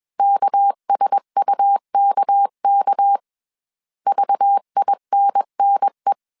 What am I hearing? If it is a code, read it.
Morse code, heard as a keyed tone, about 21 wpm, 800 Hz. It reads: KSUKK VINNE
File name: CW / DET / 1 6 1500C S C 20wpm 2116